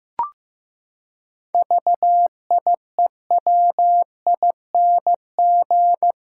Morse code, heard as a keyed tone, about 15 wpm, 700 Hz. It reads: VIEWING
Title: Pangram Challenge v2 - 15wpm